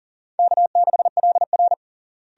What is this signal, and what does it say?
Morse code, heard as a keyed tone, about 40 wpm, 700 Hz. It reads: K6LR
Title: USA Call Signs - 4 Characters - Repeated 3x - 40wpm